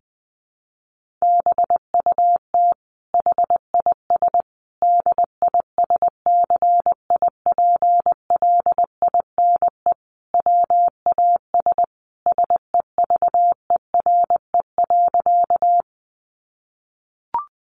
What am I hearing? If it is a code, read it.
Morse code, heard as a keyed tone, about 20 wpm, 700 Hz. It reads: BUT HIS DISCIPLINE WAS SEVERE.
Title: Sentences from Top 1400 Words 20wpm